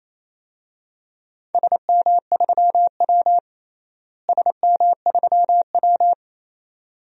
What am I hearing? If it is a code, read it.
Morse code, heard as a keyed tone, about 28 wpm, 700 Hz. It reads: SM3W SM3W
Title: DXCC-100 Call Signs - 4 Characters - Repeated 2x - 28wpm